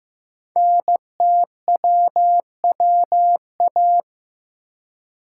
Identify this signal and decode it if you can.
Morse code, heard as a keyed tone, about 15 wpm, 700 Hz. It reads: NTWWA
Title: Code Groups - 5 Letters - 15wpm